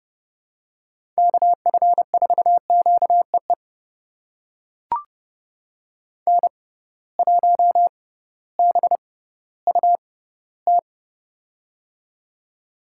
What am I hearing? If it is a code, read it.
Morse code, heard as a keyed tone, about 30 wpm, spaced as for 16 wpm, 700 Hz. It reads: KF4QEE N1BUT
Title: USA Call Signs - 6x Character Spacing - 30wpm